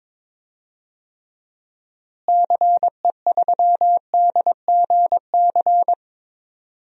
Morse code, heard as a keyed tone, about 22 wpm, 700 Hz. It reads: CE3DGC
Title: DXCC-100 Call Signs - 6 Characters - Repeated 2x - 22wpm